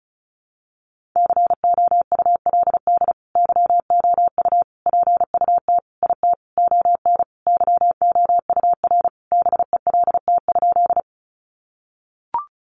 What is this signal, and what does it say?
Morse code, heard as a keyed tone, about 34 wpm, 700 Hz. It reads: COULD YOU PUT IT ON YOUR BELT?
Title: Sentences from Top 1400 Words 35wpm